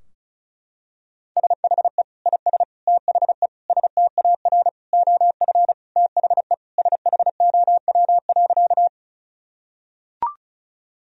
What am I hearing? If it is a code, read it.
Morse code, heard as a keyed tone, about 34 wpm, 700 Hz. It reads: SHE IS THE STAR OF THE SHOW.
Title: Sentences from Top 1000 Words - Review Part 2 - 35wpm